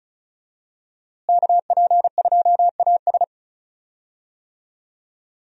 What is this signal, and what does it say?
Morse code, heard as a keyed tone, about 34 wpm, 700 Hz. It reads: KP2AS
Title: USA Call Signs - 5 Characters - Repeated 3x - 35wpm